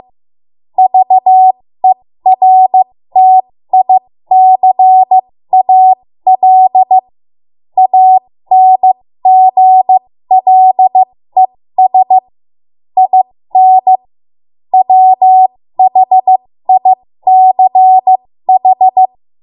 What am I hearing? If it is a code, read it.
Morse code, heard as a keyed tone, about 15 wpm, spaced as for 13 wpm, 750 Hz. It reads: VERTICAL ANGLES IN WHICH
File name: Morse / 13WPM